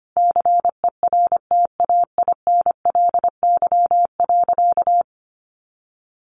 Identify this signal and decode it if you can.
Morse code, heard as a keyed tone, about 25 wpm, 700 Hz. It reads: CERTAINLY.
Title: Sentences from Top 900 Words 25wpm